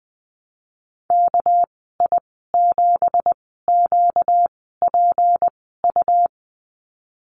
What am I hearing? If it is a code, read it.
Morse code, heard as a keyed tone, about 20 wpm, spaced as for 15 wpm, 700 Hz. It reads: KI7QPU
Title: 245 Lesson 39g - Letter z - Call Signs 20wpm